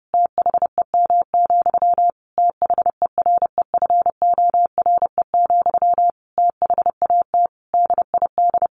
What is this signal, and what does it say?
Morse code, heard as a keyed tone, about 30 wpm, 700 Hz. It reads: THEM, THEREFORE, THAT DID
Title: Sentences from Top 800 Words 30wpm